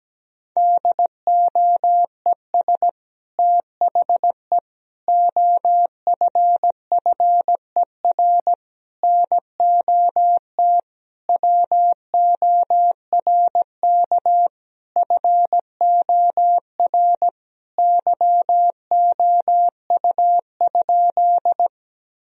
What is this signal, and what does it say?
Morse code, heard as a keyed tone, about 17 wpm, 700 Hz. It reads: DOES THE OFFER NOT WORK FOR YOU?
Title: Sentences from Top 1000 Words - Review Part 4 - 17wpm